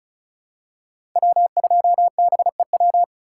Morse code, heard as a keyed tone, about 34 wpm, 700 Hz. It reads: W2BEW
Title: USA Call Signs - 5 Characters - 2x Character Spacing - 35wpm